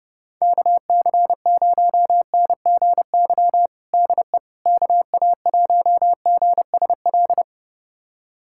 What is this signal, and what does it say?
Morse code, heard as a keyed tone, about 30 wpm, 700 Hz. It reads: KC0NGY DE KA1GSL
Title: USA Callsign DE Callsign - 6 Characters - 30wpm